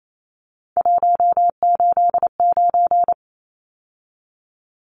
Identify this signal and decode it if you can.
Morse code, heard as a keyed tone, about 28 wpm, 700 Hz. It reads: 189
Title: Numbers - 3-Digit 28wpm